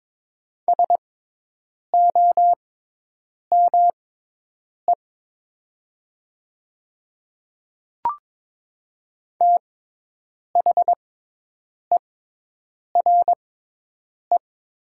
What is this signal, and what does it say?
Morse code, heard as a keyed tone, about 22 wpm, spaced as for 8 wpm, 700 Hz. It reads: SOME THERE
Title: 6x Character Spacing - Top 100 Words - 22wpm